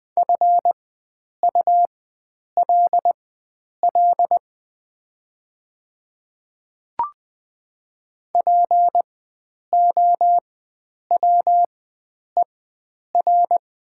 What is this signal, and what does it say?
Morse code, heard as a keyed tone, about 20 wpm, spaced as for 9 wpm, 700 Hz. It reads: FULL POWER
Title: 4x Character Spacing - Top 500 Words - 20wpm